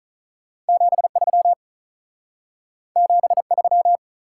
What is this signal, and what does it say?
Morse code, heard as a keyed tone, about 43 wpm, 700 Hz. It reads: 73 73
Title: Speed-Racing Top 180 QSO Elements 35wpm